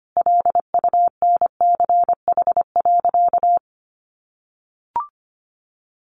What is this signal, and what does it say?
Morse code, heard as a keyed tone, about 25 wpm, 700 Hz. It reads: LUNCH.
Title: Easy Phrases 25wpm